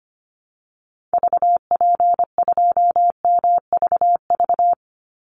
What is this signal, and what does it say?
Morse code, heard as a keyed tone, about 25 wpm, 700 Hz. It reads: VP2MVV
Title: Call Signs Repeated 3x - 25wpm